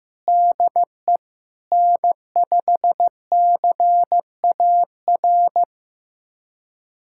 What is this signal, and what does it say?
Morse code, heard as a keyed tone, about 15 wpm, 700 Hz. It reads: DE N5CAR
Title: USA Callsign DE Callsign 15wpm